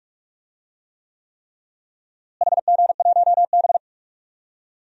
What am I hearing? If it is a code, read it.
Morse code, heard as a keyed tone, about 46 wpm, 700 Hz. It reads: HG1B
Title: DXCC-100 Call Signs - 4 Characters - Repeated 3x - 45wpm